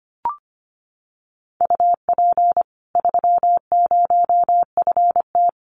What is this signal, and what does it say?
Morse code, heard as a keyed tone, about 25 wpm, 700 Hz. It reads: UP 30FT
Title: Common QSO Phrases 25wpm